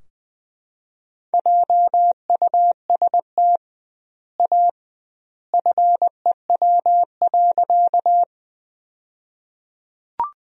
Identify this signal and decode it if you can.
Morse code, heard as a keyed tone, about 20 wpm, 700 Hz. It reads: JUST A FEW.
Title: 2x Word-Spacing - Sentences from Top 300 Words - 20wpm